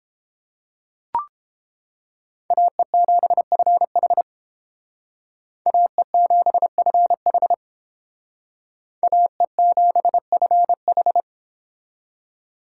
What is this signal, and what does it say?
Morse code, heard as a keyed tone, about 30 wpm, 700 Hz. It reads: AE7FH AE7FH AE7FH